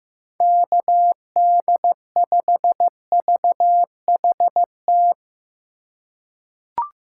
Morse code, heard as a keyed tone, about 15 wpm, 700 Hz. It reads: KD5VHT
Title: USA Call Signs - 6 Characters - 3x Character Spacing - 15wpm